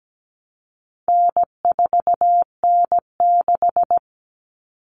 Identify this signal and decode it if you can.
Morse code, heard as a keyed tone, about 17 wpm, 700 Hz. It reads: N4N6
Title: Common Cut Numbers - 4-Digit 17wpm